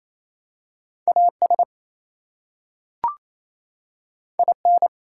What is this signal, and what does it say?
Morse code, heard as a keyed tone, about 27 wpm, 700 Hz. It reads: AS IN
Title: Two Letter Words - 28wpm